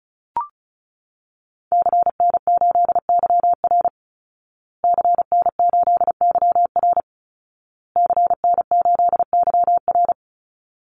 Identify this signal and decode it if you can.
Morse code, heard as a keyed tone, about 34 wpm, 700 Hz. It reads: CN8YR CN8YR CN8YR
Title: Call Signs Repeated 2x - 35wpm